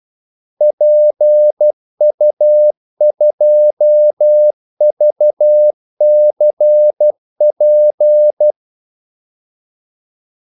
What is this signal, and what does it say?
Morse code, heard as a keyed tone, about 12 wpm, 600 Hz. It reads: PU2VCP
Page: 2 ekstra ordmellemrum mellem kaldesignalerne